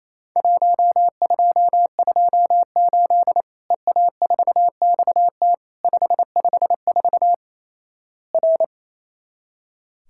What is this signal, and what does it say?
Morse code, heard as a keyed tone, about 28 wpm, 700 Hz. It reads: 1228 EA4XT 554 R
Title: Worked All Europe DX Contest - QTC Exchange - 28wpm